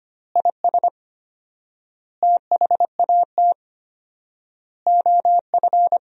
Morse code, heard as a keyed tone, about 25 wpm, 700 Hz. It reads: IS THAT OF
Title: 4x Word-Spacing - Sentences from Top 500 Words - 25wpm